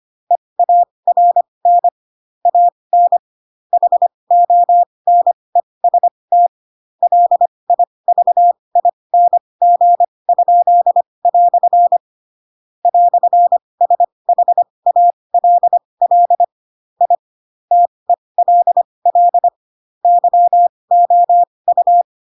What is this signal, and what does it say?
Morse code, heard as a keyed tone, about 25 wpm, spaced as for 20 wpm, 700 Hz. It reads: EARN AN HONEST LIVING?" "SHALL I TELL YOU
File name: Pinocchio / Pinocchio ch4 15 25 ik1hge